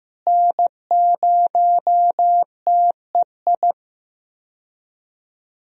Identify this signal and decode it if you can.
Morse code, heard as a keyed tone, about 15 wpm, 700 Hz. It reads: N0TEI
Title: USA Call Signs - 5 Characters - Repeated 3x - 15wpm